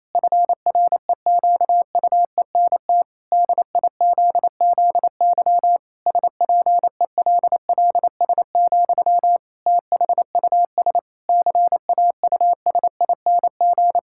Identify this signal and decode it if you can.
Morse code, heard as a keyed tone, about 28 wpm, 700 Hz. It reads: FREQUENT DIZZY SPELLS, THUS CAUSING
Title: Pangram Challenge v2 - 28wpm